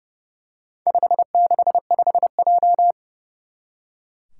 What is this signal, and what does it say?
Morse code, heard as a keyed tone, about 30 wpm, 700 Hz. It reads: 565J
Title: Four Challenging Letter-Number Combinations 30wpm